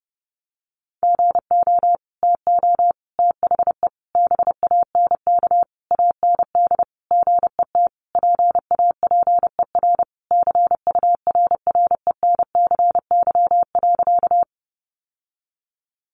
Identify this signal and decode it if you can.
Morse code, heard as a keyed tone, about 30 wpm, 700 Hz. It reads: GO TO THE BANK AND GET PAPER CURRENCY.
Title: Sentences from Top 1500 Words 30wpm